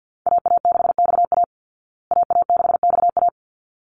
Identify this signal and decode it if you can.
Morse code, heard as a keyed tone, about 50 wpm, 700 Hz. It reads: AA6XA AA6XA
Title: Call Signs Repeated 3x - 50wpm